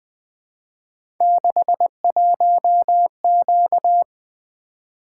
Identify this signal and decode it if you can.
Morse code, heard as a keyed tone, about 20 wpm, 700 Hz. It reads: 61Q
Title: Three Letter-Number Combinations - 20wpm